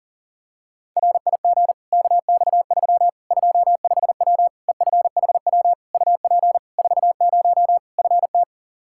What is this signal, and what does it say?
Morse code, heard as a keyed tone, about 40 wpm, 700 Hz. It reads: RIG KX3 25W EFHW UP 40 FT
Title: Practice QSO - Exchange 2 - Info - 40wpm